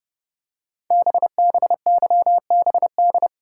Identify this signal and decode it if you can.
Morse code, heard as a keyed tone, about 30 wpm, 700 Hz. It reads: BBYBD
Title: Code Groups - 5 Letters - 30wpm